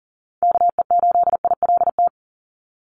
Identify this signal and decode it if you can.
Morse code, heard as a keyed tone, about 40 wpm, 700 Hz. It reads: KE8ILT
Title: USA Call Signs 40wpm